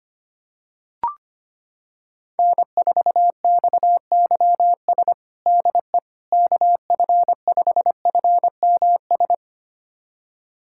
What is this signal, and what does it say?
Morse code, heard as a keyed tone, about 25 wpm, 700 Hz. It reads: N4XYS DE KF5FMS
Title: USA Callsign DE Callsign 25wpm